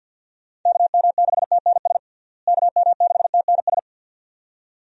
Morse code, heard as a keyed tone, about 53 wpm, 700 Hz. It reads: KM6TNS KM6TNS
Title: USA Call Signs - 6 Characters - Repeated 2x - 50wpm